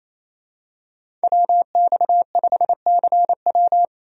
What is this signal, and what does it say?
Morse code, heard as a keyed tone, about 28 wpm, 700 Hz. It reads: WX5CW
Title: USA Call Signs - 5 Characters - Repeated 2x - 28wpm